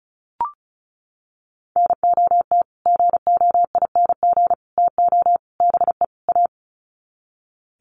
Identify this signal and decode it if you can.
Morse code, heard as a keyed tone, about 34 wpm, 700 Hz. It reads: NOT GOING TO BE A
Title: Sets of 5 Words Encore 35wpm